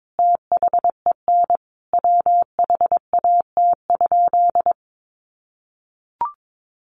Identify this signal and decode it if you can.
Morse code, heard as a keyed tone, about 22 wpm, 700 Hz. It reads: THEN WHAT?
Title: Sentences from Top 1500 Words 22wpm